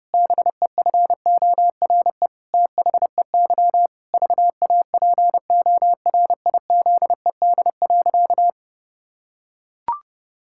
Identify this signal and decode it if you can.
Morse code, heard as a keyed tone, about 30 wpm, 700 Hz. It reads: BEFORE THEY VAPORIZED.
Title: Pangram Challenge v2 - 30wpm